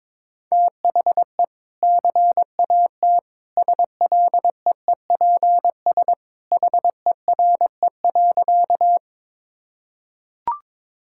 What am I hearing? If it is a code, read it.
Morse code, heard as a keyed tone, about 22 wpm, 700 Hz. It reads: THE CAT SLEEPS HERE.
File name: practice / 22wpm / Easy Sentences with Hints - 22wpm